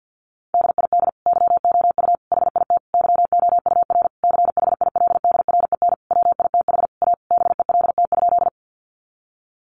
Morse code, heard as a keyed tone, about 50 wpm, 700 Hz. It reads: DID YOU HIT YOUR CHILDREN WITH A BELT?
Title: Sentences from Top 1400 Words 50wpm